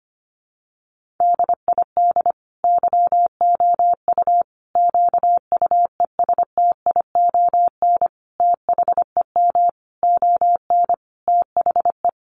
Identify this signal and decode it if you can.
Morse code, heard as a keyed tone, about 25 wpm, 700 Hz. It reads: DID YOU QUESTION THEM ON THE
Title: Sentences from Top 800 Words 25wpm